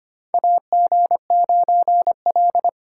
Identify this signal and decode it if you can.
Morse code, heard as a keyed tone, about 25 wpm, 690 Hz. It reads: AG9L
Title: USA Call Signs - 4 Characters - 4x Character Spacing - 25wpm